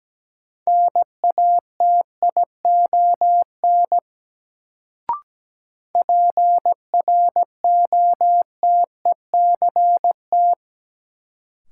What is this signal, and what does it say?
Morse code, heard as a keyed tone, about 17 wpm, 700 Hz. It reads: NATION PROTECT
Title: Sentences from Top 900 Words - New Word Drill - 17wpm